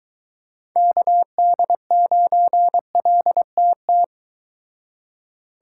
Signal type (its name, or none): Morse code